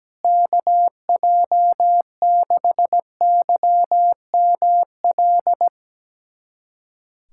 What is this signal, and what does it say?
Morse code, heard as a keyed tone, about 17 wpm, 700 Hz. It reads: KJ6YML
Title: USA Callsign DE Callsign - 6 Characters - 17wpm